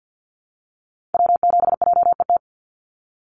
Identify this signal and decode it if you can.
Morse code, heard as a keyed tone, about 50 wpm, 700 Hz. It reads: W7JET
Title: NA SOTA Activator Callsigns 50wpm